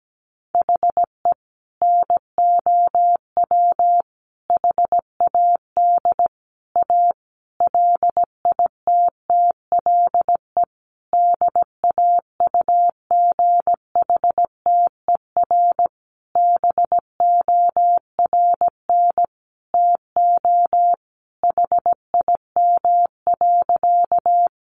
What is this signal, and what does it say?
Morse code, heard as a keyed tone, about 17 wpm, 700 Hz. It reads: HE NOW HAD A LITTLE DAUGHTER BORN TO HIM.
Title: Sentences from Top 1500 Words 17wpm